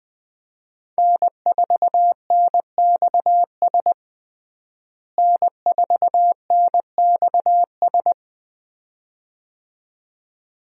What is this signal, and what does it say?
Morse code, heard as a keyed tone, about 21 wpm, 700 Hz. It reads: N4NXS N4NXS
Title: USA Call Signs - 5 Characters - Repeated 2x - 20wpm